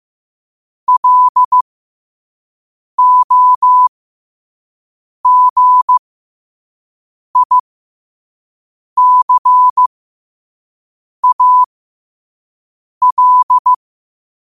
Morse code, heard as a keyed tone, about 15 wpm, spaced as for 5 wpm, 1000 Hz. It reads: LOGICAL
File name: qotdfiles / 5wpm